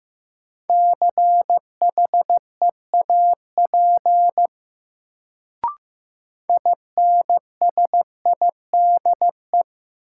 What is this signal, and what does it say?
Morse code, heard as a keyed tone, about 15 wpm, 700 Hz. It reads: CHEAP INSIDE
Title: Sentences from Top 1200 Words - New Word Drill Alt - 15wpm